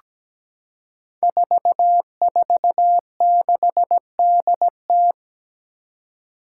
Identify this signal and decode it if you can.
Morse code, heard as a keyed tone, about 17 wpm, 700 Hz. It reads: 446DT
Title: Five Challenging Letter-Number Combinations 17wpm